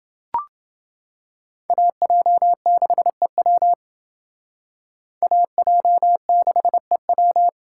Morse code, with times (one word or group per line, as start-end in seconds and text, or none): 1.7-3.7 AJ6EW
5.2-7.5 AJ6EW